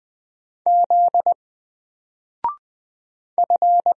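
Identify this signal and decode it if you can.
Morse code, heard as a keyed tone, about 20 wpm, 700 Hz. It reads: Z F
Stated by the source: Single letter and number Morse drill at 20 wpm, from N4LCD.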